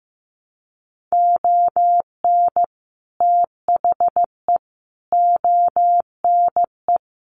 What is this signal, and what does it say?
Morse code, heard as a keyed tone, about 15 wpm, 700 Hz. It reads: ON THE ONE
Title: Sets of 5 Words Encore2 15wpm